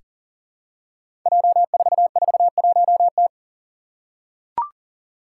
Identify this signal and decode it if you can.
Morse code, heard as a keyed tone, about 37 wpm, 700 Hz. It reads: J441T
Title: Five Challenging Letter-Number Combinations 40wpm